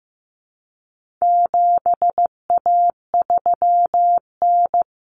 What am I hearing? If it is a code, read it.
Morse code, heard as a keyed tone, about 15 wpm, 700 Hz. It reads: 7A3N
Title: Common Cut Numbers - 4-Digit 15wpm